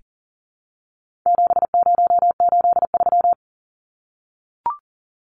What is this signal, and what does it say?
Morse code, heard as a keyed tone, about 40 wpm, 700 Hz. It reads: 7083
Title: Numbers - 4-Digit 40wpm